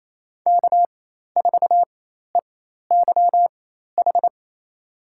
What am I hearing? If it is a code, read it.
Morse code, heard as a keyed tone, about 27 wpm, spaced as for 13 wpm, 700 Hz. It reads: K4EYH
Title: USA Call Signs - 4x Character Spacing - 28wpm